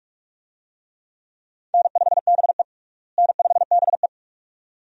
Morse code, heard as a keyed tone, about 46 wpm, 700 Hz. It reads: N5BE N5BE
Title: USA Call Signs - 4 Characters - Repeated 2x - 45wpm